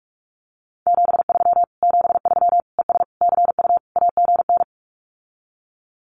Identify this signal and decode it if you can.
Morse code, heard as a keyed tone, about 44 wpm, 700 Hz. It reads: 73 73 ES CU AGN
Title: Common QSO Phrases Encore 45wpm